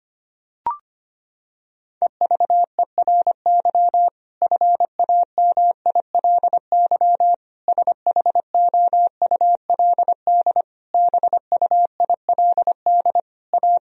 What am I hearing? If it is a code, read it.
Morse code, heard as a keyed tone, about 25 wpm, 700 Hz. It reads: EVERY FAMILY SHOULD BUILD A
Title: Sentences from Top 1000 Words - Review Part 1 - 25wpm